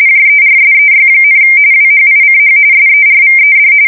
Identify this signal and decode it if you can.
RTTY, about 45.45 baud. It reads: JERSEY, WHICH SERVED
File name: fsk170 / USS-New-Jersey